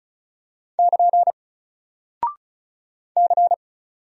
Morse code, heard as a keyed tone, about 36 wpm, 700 Hz. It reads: <KN> C
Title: Single Character (Letter:Number:Punctuation:Prosign) 35wpm